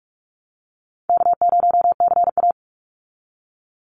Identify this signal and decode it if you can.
Morse code, heard as a keyed tone, about 44 wpm, 700 Hz. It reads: K0CA
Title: Call Signs Repeated 2x - 45wpm